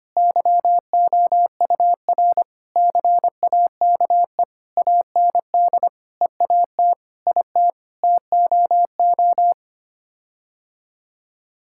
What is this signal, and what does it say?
Morse code, heard as a keyed tone, about 25 wpm, 700 Hz. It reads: YOUR CAKE AND EAT IT TOO
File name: Sets of 9 Words - 25wpm